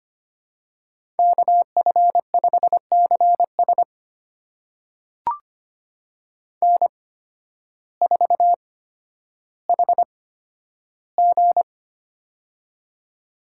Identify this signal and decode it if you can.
Morse code, heard as a keyed tone, about 25 wpm, spaced as for 12 wpm, 700 Hz. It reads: KF5CS N4HG
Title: USA Call Signs - 8x Character Spacing - 25wpm